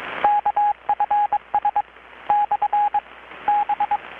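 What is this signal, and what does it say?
Morse code, heard as a keyed tone, about 21 wpm, 830 Hz, with radio noise. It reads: KFS / B
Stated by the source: Morse from KPH's Night of Nights XX maritime radiotelegraphy event, received in Chiapas, Mexico about 3,800 km away on a Kenwood R-600 with a grounded 10 m long-wire antenna.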